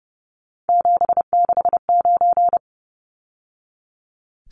Morse code, heard as a keyed tone, about 30 wpm, 700 Hz. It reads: 769
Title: Numbers - 3-Digit 30wpm